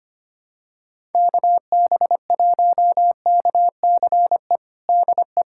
Morse code, heard as a keyed tone, about 25 wpm, 700 Hz. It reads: KB1KCE DE
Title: USA Callsign DE Callsign 25wpm